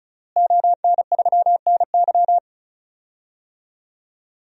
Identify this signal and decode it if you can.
Morse code, heard as a keyed tone, about 34 wpm, 700 Hz. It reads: ON3NY
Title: DXCC-100 Call Signs - 5 Characters - Repeated 3x - 35wpm